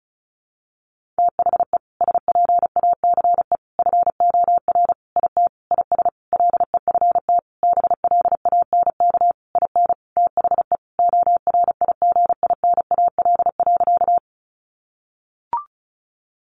Morse code, heard as a keyed tone, about 34 wpm, 700 Hz. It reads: THE SPACE FOR IT IS LEFT BLANK IN THE ORIGINAL.
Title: Sentences from Top 1500 Words 35wpm